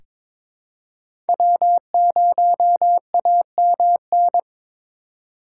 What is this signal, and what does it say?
Morse code, heard as a keyed tone, about 22 wpm, 700 Hz. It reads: W0AMN
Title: USA Call Signs - 5 Characters - 2x Character Spacing - 22wpm